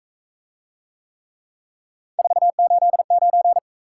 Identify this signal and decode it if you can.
Morse code, heard as a keyed tone, about 40 wpm, 700 Hz. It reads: 489